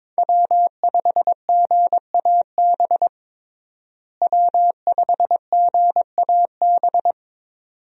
Morse code, heard as a keyed tone, about 22 wpm, 700 Hz. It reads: W5GAB W5GAB
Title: USA Call Signs - 5 Characters - Repeated 2x - 22wpm